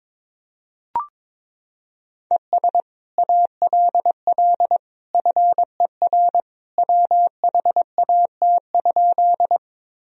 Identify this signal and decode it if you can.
Morse code, heard as a keyed tone, about 22 wpm, 700 Hz. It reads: ES ALL FER WHAT?
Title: Sentences from Top 100 Words and Abbreviations 22wpm